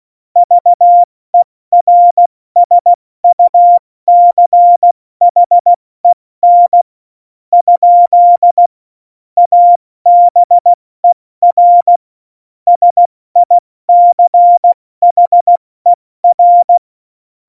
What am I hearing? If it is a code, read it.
Morse code, heard as a keyed tone, about 16 wpm, 700 Hz. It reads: VERSUCHEN ? ABER SICHER
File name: Morsekurs / Witze / Witz 13 March 2018, 70 F80